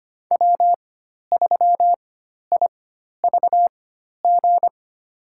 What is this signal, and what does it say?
Morse code, heard as a keyed tone, about 25 wpm, spaced as for 12 wpm, 700 Hz. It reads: W3IVG
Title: USA Call Signs - 5 Characters - 4x Character Spacing - 25wpm